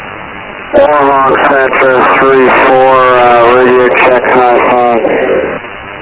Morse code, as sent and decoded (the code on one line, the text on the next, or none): none